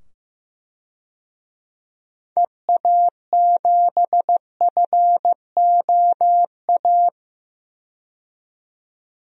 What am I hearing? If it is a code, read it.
Morse code, heard as a keyed tone, about 15 wpm, 700 Hz. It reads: EA7FOA
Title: DXCC-100 Call Signs - 6 Characters - 15wpm